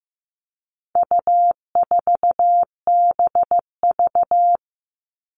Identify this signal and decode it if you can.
Morse code, heard as a keyed tone, about 15 wpm, 700 Hz. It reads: U4BV
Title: All Cut Numbers - 4-Digit 15wpm